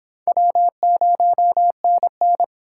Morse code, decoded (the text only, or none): W0NN